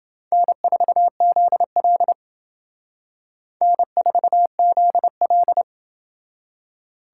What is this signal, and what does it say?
Morse code, heard as a keyed tone, about 28 wpm, 700 Hz. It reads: N4ZL N4ZL